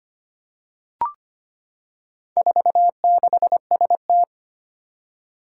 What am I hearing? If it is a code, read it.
Morse code, heard as a keyed tone, about 25 wpm, 700 Hz. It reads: 46ST
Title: Four Challenging Letter-Number Combinations 25wpm